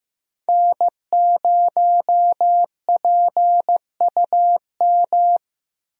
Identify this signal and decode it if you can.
Morse code, heard as a keyed tone, about 15 wpm, 700 Hz. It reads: N0PUM
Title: USA Call Signs - 5 Characters - Repeated 2x - 15wpm